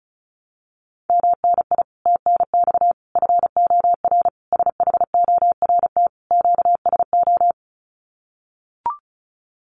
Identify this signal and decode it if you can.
Morse code, heard as a keyed tone, about 34 wpm, 700 Hz. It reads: MNI TNX FOR SHORT QSO
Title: Common QSO Phrases 35wpm